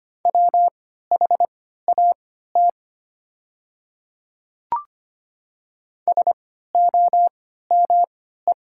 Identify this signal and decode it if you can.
Morse code, heard as a keyed tone, about 25 wpm, spaced as for 14 wpm, 700 Hz. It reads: WHAT SOME
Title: 3x Character Spacing - Top 100 Words - 25wpm